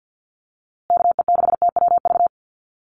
Morse code, heard as a keyed tone, about 50 wpm, 700 Hz. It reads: KE6TWV
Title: USA Call Signs Repeated 2x 50wpm